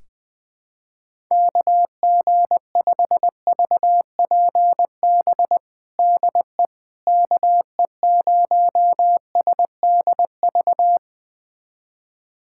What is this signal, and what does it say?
Morse code, heard as a keyed tone, about 20 wpm, 700 Hz. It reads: KG5VPB DE KE0SDV
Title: USA Callsign DE Callsign 20wpm